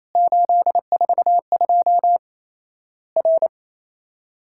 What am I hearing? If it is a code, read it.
Morse code, heard as a keyed tone, about 28 wpm, 700 Hz. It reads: 842 R
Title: Worked All Europe DX Contest - QTC Exchange - 28wpm